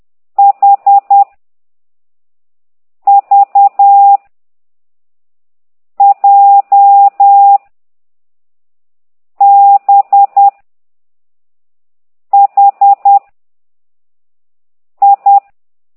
Morse code, heard as a keyed tone, about 10 wpm, 800 Hz. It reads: H V J B H I
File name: ct2idw / exerc6